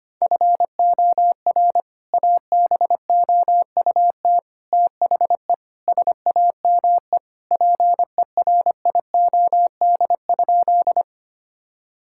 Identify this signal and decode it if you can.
Morse code, heard as a keyed tone, about 25 wpm, 700 Hz. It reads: FOR ABOUT THE SAME PERIOD?
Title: Sentences from Top 1000 Words - Review Part 5 - 25wpm